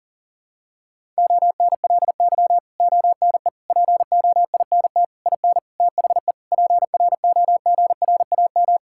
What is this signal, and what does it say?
Morse code, heard as a keyed tone, about 40 wpm, 700 Hz. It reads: ONLY ONE POINT IN THE PROGRAM
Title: Sentences from Top 1000 Words - Review Part 5 - 40wpm